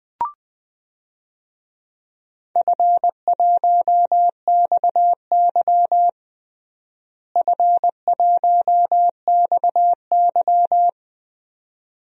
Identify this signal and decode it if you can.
Morse code, heard as a keyed tone, about 20 wpm, 700 Hz. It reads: F1XY F1XY
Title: DXCC-100 Call Signs - 4 Characters - Repeated 2x - 20wpm